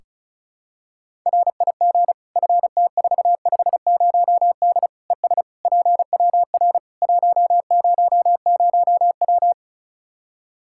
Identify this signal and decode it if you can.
Morse code, heard as a keyed tone, about 34 wpm, 700 Hz. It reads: RIG FT450D ES PWR 100W
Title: K1USN SST QSO Practice 35wpm